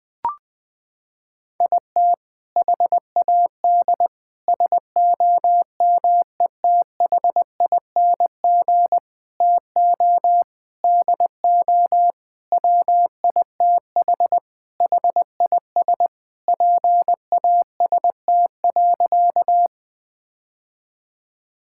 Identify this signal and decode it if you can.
Morse code, heard as a keyed tone, about 20 wpm, 700 Hz. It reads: IT HAD SOMETHING TO DO WITH HIS PAST.
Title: Sentences with 5-Word Phrases - 20wpm